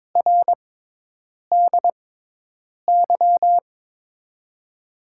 Morse code, heard as a keyed tone, about 22 wpm, spaced as for 8 wpm, 700 Hz. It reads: RDY
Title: USA Call Signs - 6x Character Spacing - 22wpm